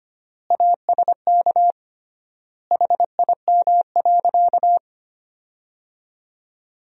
Morse code, heard as a keyed tone, about 25 wpm, 700 Hz. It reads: ASK HIM.
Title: 3x Word-Spacing - Sentences from Top 300 Words - 25wpm